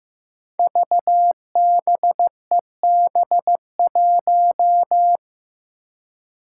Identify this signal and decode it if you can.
Morse code, heard as a keyed tone, about 15 wpm, 700 Hz. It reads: VBEB1
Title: Five Challenging Letter-Number Combinations 15wpm